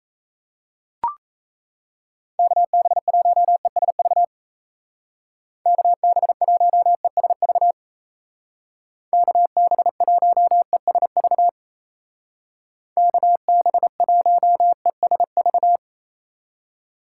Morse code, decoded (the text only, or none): KB1ESV KB1ESV KB1ESV KB1ESV